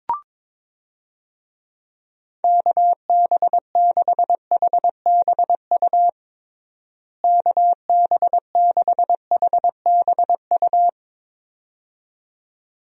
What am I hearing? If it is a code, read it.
Morse code, heard as a keyed tone, about 22 wpm, 700 Hz. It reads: KB6HBU KB6HBU
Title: USA Call Signs - 6 Characters - Repeated 2x - 22wpm